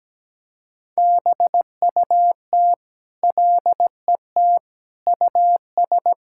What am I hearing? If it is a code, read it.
Morse code, heard as a keyed tone, about 17 wpm, 700 Hz. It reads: BUT LET US
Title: Sentences from Top 1600 Words 17wpm